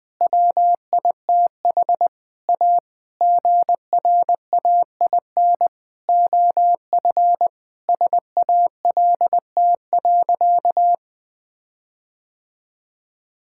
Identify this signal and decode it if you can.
Morse code, heard as a keyed tone, about 21 wpm, 700 Hz. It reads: WITH A GRAIN OF SALT.
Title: Sentences with 7-Word Phrases - Spoken Beginning - 20wpm